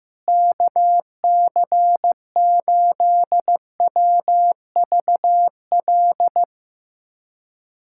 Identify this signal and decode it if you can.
Morse code, heard as a keyed tone, about 15 wpm, 700 Hz. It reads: KC8WVL
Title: USA Callsign DE Callsign 15wpm